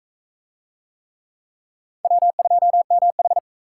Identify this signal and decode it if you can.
Morse code, heard as a keyed tone, about 42 wpm, 700 Hz. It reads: W2MH